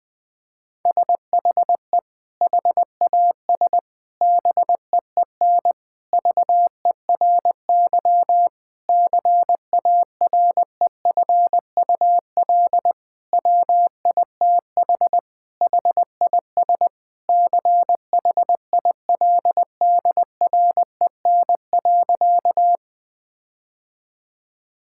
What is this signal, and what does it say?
Morse code, heard as a keyed tone, about 20 wpm, 700 Hz. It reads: SHE HAS BEEN VERY CAREFUL WITH HIS CHILDREN.
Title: Sentences from Top 1600 Words 20wpm